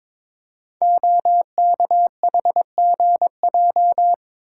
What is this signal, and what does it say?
Morse code, heard as a keyed tone, about 22 wpm, 700 Hz. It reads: OKHGJ
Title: Code Groups - 5 Letter:Numbers - 22wpm